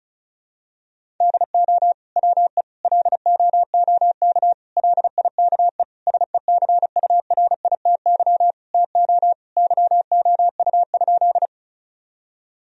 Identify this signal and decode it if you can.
Morse code, heard as a keyed tone, about 36 wpm, 700 Hz. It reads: DO WE LOOK LIKE SECURITY TO YOU?
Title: Sentences from Top 1000 Words - Review Part 4 - 35wpm